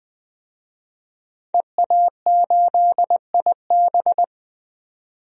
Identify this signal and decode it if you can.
Morse code, heard as a keyed tone, about 20 wpm, 700 Hz. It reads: EA8IB
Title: DXCC-100 Call Signs - 5 Characters - 20wpm